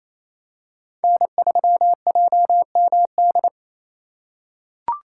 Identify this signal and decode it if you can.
Morse code, heard as a keyed tone, about 28 wpm, 700 Hz. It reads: N3JMD
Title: USA Call Signs - 5 Characters - 6x Character Spacing - 28wpm